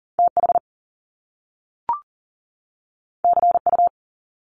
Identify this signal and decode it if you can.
Morse code, heard as a keyed tone, about 38 wpm, 700 Hz. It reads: TH CU
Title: Top Two Letter Combinations in Words 40wpm